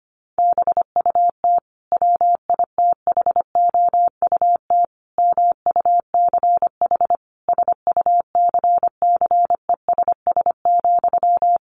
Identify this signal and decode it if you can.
Morse code, heard as a keyed tone, about 25 wpm, 700 Hz. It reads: BUT WITHOUT MUCH SUCCESS,
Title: Sentences from Top 900 Words 25wpm